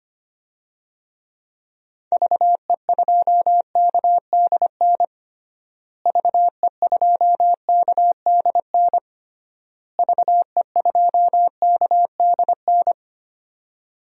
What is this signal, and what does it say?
Morse code, heard as a keyed tone, about 25 wpm, 700 Hz. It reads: VE2KDN VE2KDN VE2KDN
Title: DXCC-100 Call Signs - 6 Characters - Repeated 3x - 25wpm